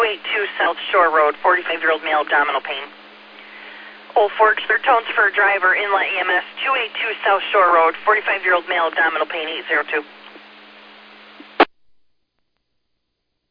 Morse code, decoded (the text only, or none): none